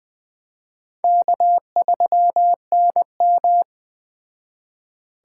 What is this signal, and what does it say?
Morse code, heard as a keyed tone, about 21 wpm, 700 Hz. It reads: K3NM
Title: Speed Racing - USA 1x2 Call Signs 17wpm